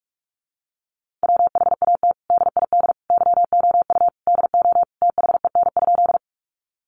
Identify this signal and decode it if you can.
Morse code, heard as a keyed tone, about 44 wpm, 700 Hz. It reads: WHAT DID YOU DO THEN?
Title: Sentences from Top 100 Words 45wpm